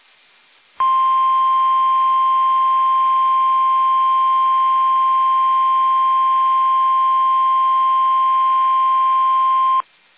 Emergency Alert System (SAME) tones (the 1050 Hz attention tone). It is heard from 0.8 to 9.9 s.